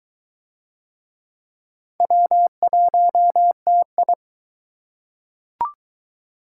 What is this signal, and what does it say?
Morse code, heard as a keyed tone, about 23 wpm, 700 Hz. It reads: W1TI